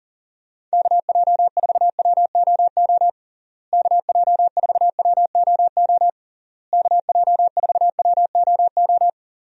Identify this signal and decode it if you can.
Morse code, heard as a keyed tone, about 40 wpm, 700 Hz. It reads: KJ4WOO KJ4WOO KJ4WOO
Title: USA Call Signs - 6 Characters - Repeated 3x - 40wpm